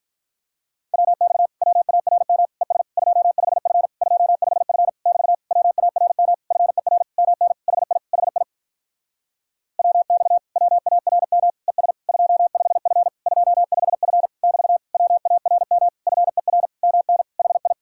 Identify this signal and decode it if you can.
Morse code, heard as a keyed tone, about 48 wpm, 700 Hz. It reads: WX WARM ES 25F 25F <BT> WARM FER MN HI HI WX WARM ES 25F 25F <BT> WARM FER MN HI